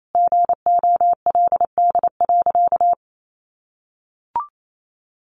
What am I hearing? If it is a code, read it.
Morse code, heard as a keyed tone, about 28 wpm, 700 Hz. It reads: GOLD.
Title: Sentences from Top 1500 Words 28wpm